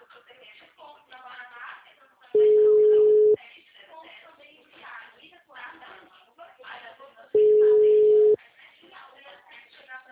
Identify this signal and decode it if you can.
Ringback tone (Europe, 425 Hz), with speech.